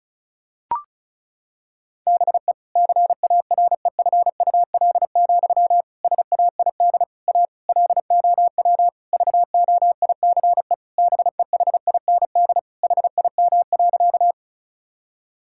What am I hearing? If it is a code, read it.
Morse code, heard as a keyed tone, about 36 wpm, 700 Hz. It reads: BE CAREFUL, SAID A LOW VOICE BEHIND HIM.
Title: Sentences from Top 1600 Words 35wpm